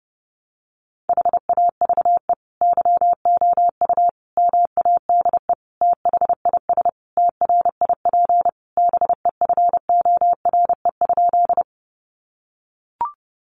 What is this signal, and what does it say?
Morse code, heard as a keyed tone, about 30 wpm, 700 Hz. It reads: HAVE YOU MADE THIS TRIP BEFORE?
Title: Sentences from Top 1300 Words 30wpm